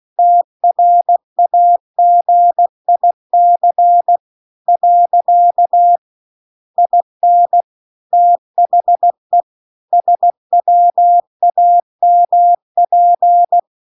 Morse code, heard as a keyed tone, about 16 wpm, 700 Hz. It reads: TRAGIC . IN THE SWAMP